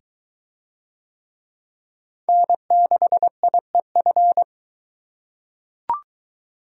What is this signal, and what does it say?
Morse code, heard as a keyed tone, about 21 wpm, 700 Hz. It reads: N6IEF